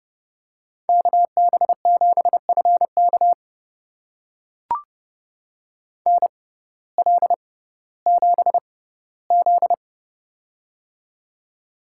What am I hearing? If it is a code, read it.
Morse code, heard as a keyed tone, about 30 wpm, spaced as for 16 wpm, 700 Hz. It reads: KB7FK NL7Z
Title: USA Call Signs - 6x Character Spacing - 30wpm